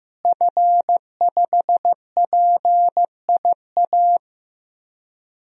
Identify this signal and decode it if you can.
Morse code, heard as a keyed tone, about 15 wpm, 700 Hz. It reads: F5PIA
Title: DXCC-100 Call Signs - 5 Characters - Repeated 3x - 15wpm